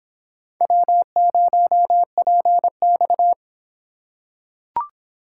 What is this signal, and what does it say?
Morse code, heard as a keyed tone, about 23 wpm, 700 Hz. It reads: W0PX